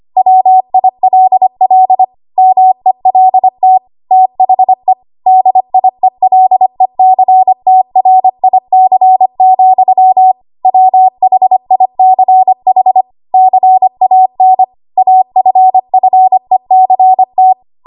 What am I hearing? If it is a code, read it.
Morse code, heard as a keyed tone, about 25 wpm, 750 Hz. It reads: WILL MELT THE DIELECTRIC, WHICH CAN AFFECT
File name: Archive / 25 WPM / 25WPM